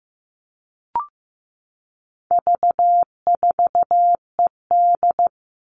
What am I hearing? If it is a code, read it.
Morse code, heard as a keyed tone, about 15 wpm, 700 Hz. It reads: V4ED
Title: All Cut Numbers - 4-Digit 15wpm